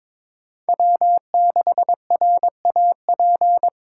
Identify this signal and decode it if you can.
Morse code, heard as a keyed tone, about 21 wpm, 700 Hz. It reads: W6RAP
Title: USA Callsign DE Callsign - 5 Characters - 22wpm